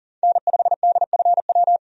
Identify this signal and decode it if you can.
Morse code, heard as a keyed tone, about 40 wpm, 700 Hz. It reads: N5DFW
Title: USA Call Signs - 5 Characters - 40wpm